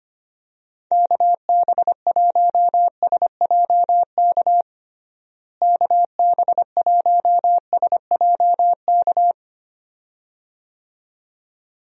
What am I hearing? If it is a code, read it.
Morse code, heard as a keyed tone, about 25 wpm, 700 Hz. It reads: KB1SJK KB1SJK
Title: USA Call Signs - 6 Characters - Repeated 2x - 25wpm